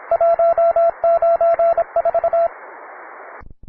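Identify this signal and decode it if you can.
Morse code, heard as a keyed tone, about 25 wpm, 670 Hz, with radio noise. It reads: 194
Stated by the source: Morse code QTC list traffic from the Worked All Europe (WAE) contest, advanced level.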